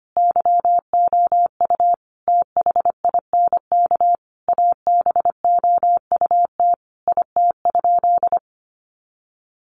Morse code, heard as a keyed tone, about 25 wpm, 700 Hz. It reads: YOU THINK ABOUT IT?
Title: Sentences from Top 200 Words 25wpm